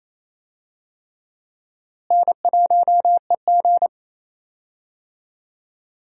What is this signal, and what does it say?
Morse code, heard as a keyed tone, about 28 wpm, 700 Hz. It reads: N1EG
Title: USA Call Signs - 4 Characters - Repeated 3x - 28wpm